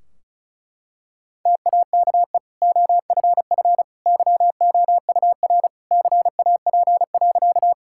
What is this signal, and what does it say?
Morse code, heard as a keyed tone, about 34 wpm, 700 Hz. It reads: TAKE OFF YOUR CAP.
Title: Sentences from Top 1600 Words 35wpm